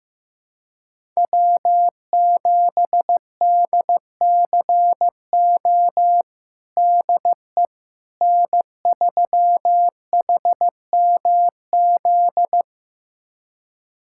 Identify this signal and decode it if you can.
Morse code, heard as a keyed tone, about 15 wpm, 700 Hz. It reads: W7DCO DE N3HMZ
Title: USA Callsign DE Callsign - 5 Characters - 15wpm